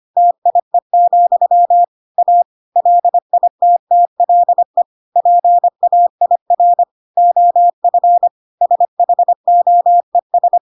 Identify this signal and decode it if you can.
Morse code, heard as a keyed tone, about 25 wpm, 700 Hz. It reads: TIE, A LITTLE PAIR OF SHOES